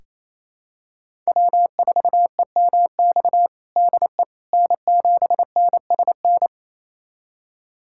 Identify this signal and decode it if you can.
Morse code, heard as a keyed tone, about 28 wpm, 700 Hz. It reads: W4EMX DE N7NSN
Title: USA Callsign DE Callsign - 5 Characters - 28wpm